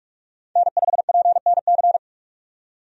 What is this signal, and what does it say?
Morse code, heard as a keyed tone, about 42 wpm, 700 Hz. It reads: N5PNC
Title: USA Call Signs - 5 Characters - 45wpm